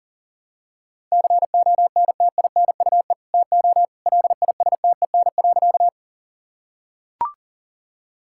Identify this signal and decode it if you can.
Morse code, heard as a keyed tone, about 40 wpm, 700 Hz. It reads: CONTINUE TO LISTEN.
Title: Sentences from Top 1000 Words - Review Part 6 - 40wpm